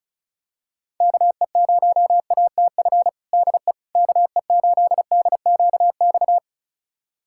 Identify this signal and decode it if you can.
Morse code, heard as a keyed tone, about 34 wpm, 700 Hz. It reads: KE0ATF DE KE8DQX
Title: USA Callsign DE Callsign 35wpm